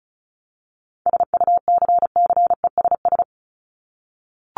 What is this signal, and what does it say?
Morse code, heard as a keyed tone, about 34 wpm, 700 Hz. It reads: SUCCESS
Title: Top 751-1000 Words 35wpm